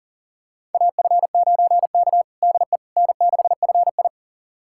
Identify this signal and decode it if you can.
Morse code, heard as a keyed tone, about 40 wpm, 700 Hz. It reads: AF9K DE N6FI
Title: USA Callsign DE Callsign - 4 Characters - 40wpm